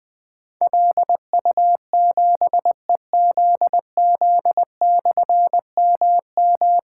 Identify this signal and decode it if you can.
Morse code, heard as a keyed tone, about 20 wpm, 700 Hz. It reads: LU7EZZ/MM
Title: Difficult Call Signs 20wpm